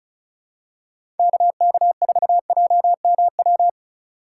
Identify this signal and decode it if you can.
Morse code, heard as a keyed tone, about 36 wpm, 700 Hz. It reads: KK4JMW
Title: USA Call Signs - 6 Characters - 4x Character Spacing - 35wpm